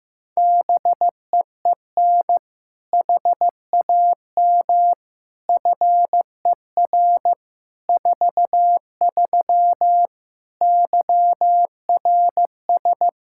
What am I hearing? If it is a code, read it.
Morse code, heard as a keyed tone, about 16 wpm, 700 Hz. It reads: BEEN HAM FER 43 YRS
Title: Practice QSO - Exchange 3 - More Info - 15wpm